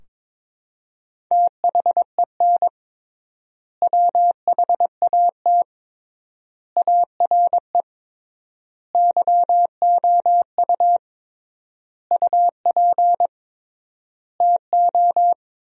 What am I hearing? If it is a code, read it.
Morse code, heard as a keyed tone, about 22 wpm, 700 Hz. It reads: THEN WHAT ARE YOU UP TO
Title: 3x Word-Spacing - Sentences from Top 100 Words - 22wpm